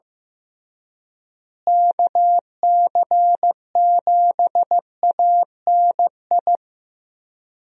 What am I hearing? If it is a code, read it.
Morse code, heard as a keyed tone, about 15 wpm, 700 Hz. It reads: KC7ANI
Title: USA Call Signs - 6 Characters - Repeated 3x - 15wpm